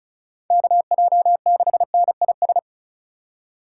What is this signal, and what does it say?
Morse code, heard as a keyed tone, about 33 wpm, 700 Hz. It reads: KJ6NIS
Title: Speed Racing - USA 2x3 Call Signs 35wpm